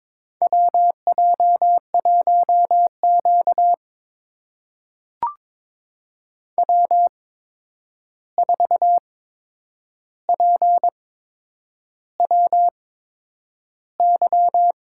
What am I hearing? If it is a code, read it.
Morse code, heard as a keyed tone, about 23 wpm, spaced as for 9 wpm, 700 Hz. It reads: WJ1Q W4PWY
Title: USA Call Signs - 8x Character Spacing - 22wpm